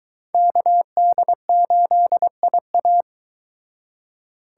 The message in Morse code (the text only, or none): KD8IA